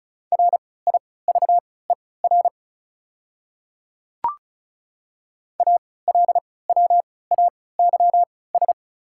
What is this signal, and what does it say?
Morse code, heard as a keyed tone, about 34 wpm, spaced as for 20 wpm, 700 Hz. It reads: RIVER ALWAYS
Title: 3x Character Spacing - Top 300 Words - 35wpm